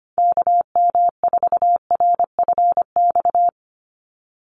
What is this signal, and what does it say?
Morse code, heard as a keyed tone, about 25 wpm, 700 Hz. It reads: KM4RFX
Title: USA Call Signs 25wpm